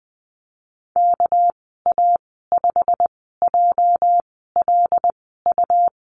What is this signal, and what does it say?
Morse code, heard as a keyed tone, about 20 wpm, spaced as for 14 wpm, 700 Hz. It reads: KA5JLU
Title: 219 Lesson 35g - Number 0 - Call Signs 20wpm